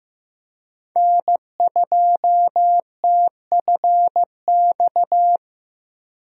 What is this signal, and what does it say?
Morse code, heard as a keyed tone, about 15 wpm, 700 Hz. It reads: N2TFX
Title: USA Call Signs - 5 Characters - Repeated 2x - 15wpm